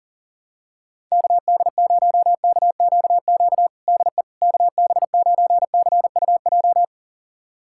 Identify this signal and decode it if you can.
Morse code, heard as a keyed tone, about 40 wpm, 700 Hz. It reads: KD0KQQ DE KB9CUJ
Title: USA Callsign DE Callsign - 6 Characters - 40wpm